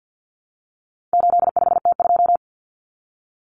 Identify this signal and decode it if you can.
Morse code, heard as a keyed tone, about 50 wpm, 700 Hz. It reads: 85T2
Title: Common Cut Numbers - 4-Digit 50wpm